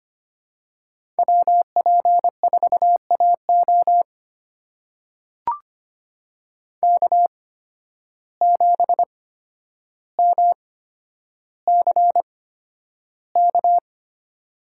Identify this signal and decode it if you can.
Morse code, heard as a keyed tone, about 25 wpm, spaced as for 11 wpm, 700 Hz. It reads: WP4AO K7MCK
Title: USA Call Signs - 5 Characters - 8x Character Spacing - 25wpm